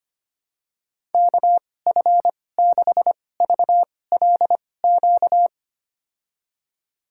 Morse code, heard as a keyed tone, about 25 wpm, spaced as for 18 wpm, 700 Hz. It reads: KF6VLQ
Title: USA Call Signs - 6 Characters - 2x Character Spacing - 25wpm